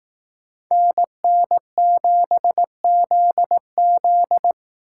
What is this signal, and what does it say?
Morse code, heard as a keyed tone, about 18 wpm, 700 Hz. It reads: NN7ZZ
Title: Speed Racing - USA 2x2 Call Signs 15wpm